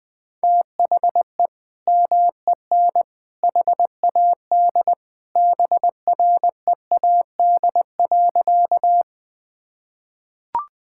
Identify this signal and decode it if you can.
Morse code, heard as a keyed tone, about 20 wpm, 700 Hz. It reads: THE MEN HAD BREAD.
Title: Sentences from Top 1600 Words 20wpm